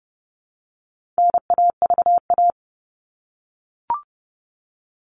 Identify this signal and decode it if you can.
Morse code, heard as a keyed tone, about 30 wpm, 700 Hz. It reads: NAVA
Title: All Cut Numbers - 4-Digit 30wpm